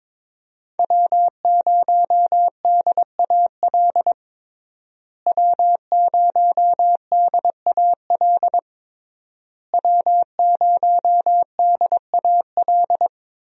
Morse code, heard as a keyed tone, about 21 wpm, 700 Hz. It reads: W0DAL W0DAL W0DAL
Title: USA Call Signs - 5 Characters - Repeated 3x - 22wpm